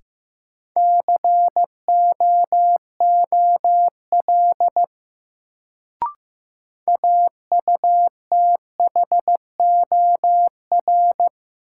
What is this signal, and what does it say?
Morse code, heard as a keyed tone, about 15 wpm, 700 Hz. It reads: COOL AUTHOR
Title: Sentences from Top 1200 Words - New Word Drill Alt - 15wpm